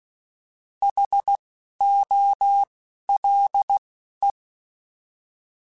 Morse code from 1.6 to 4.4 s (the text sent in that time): O L E